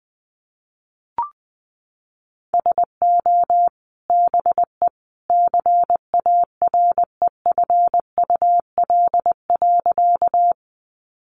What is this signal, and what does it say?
Morse code, heard as a keyed tone, about 19 wpm, 700 Hz. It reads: SO BE CAREFUL.
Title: A Sound Of Thunder 20wpm